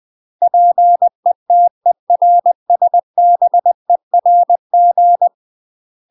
Morse code, heard as a keyed tone, about 20 wpm, 700 Hz. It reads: PETERSBERG
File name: UK MASTER QSOs TRAINING / 20WPM / UK MASTER QSOs 0068